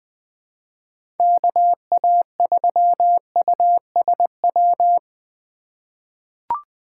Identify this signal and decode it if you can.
Morse code, heard as a keyed tone, about 20 wpm, 700 Hz. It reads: KA3USW
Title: USA Call Signs - 6 Characters - 2x Character Spacing - 20wpm